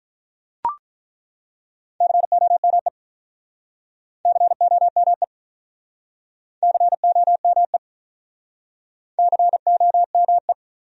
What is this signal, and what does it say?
Morse code, heard as a keyed tone, about 43 wpm, 700 Hz. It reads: COME COME COME COME
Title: Speed-Racing Top 100 Words 35wpm